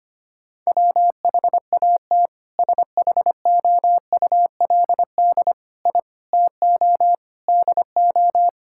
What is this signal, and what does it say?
Morse code, heard as a keyed tone, about 25 wpm, 700 Hz. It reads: WHAT SHOULD I TO DO
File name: practice / 25wpm / Sentences with 4-Word Phrases - 25wpm